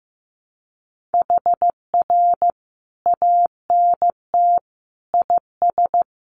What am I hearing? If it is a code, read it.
Morse code, heard as a keyed tone, about 15 wpm, 700 Hz. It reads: HR ANT IS
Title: Common QSO Phrases 15wpm